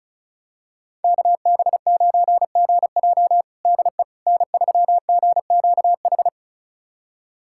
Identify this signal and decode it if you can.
Morse code, heard as a keyed tone, about 36 wpm, 700 Hz. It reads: KB9GJ DE N3GQH
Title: USA Callsign DE Callsign - 5 Characters - 35wpm